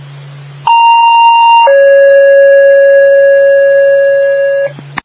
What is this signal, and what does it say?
Two-tone page, heard as a one-second tone, then a three-second tone, about 930 then 550 Hz. It is heard from 0.7 to 4.7 s.
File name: Northville